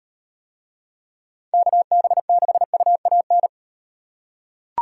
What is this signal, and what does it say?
Morse code, heard as a keyed tone, about 38 wpm, 700 Hz. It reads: KB6UAN